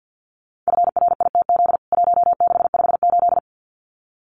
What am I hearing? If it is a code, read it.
Morse code, heard as a keyed tone, about 48 wpm, 700 Hz. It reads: FRITZ 1658
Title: CWT Exchange 50wpm